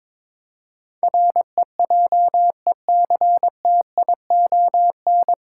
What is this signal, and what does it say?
Morse code, heard as a keyed tone, about 22 wpm, 700 Hz. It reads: REJECTION
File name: 9 Letter Words - First 3 Letters Given - 22wpm